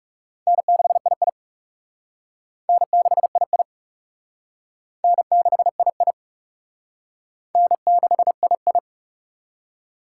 Morse code, heard as a keyed tone, about 38 wpm, 700 Hz. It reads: N6II N6II N6II N6II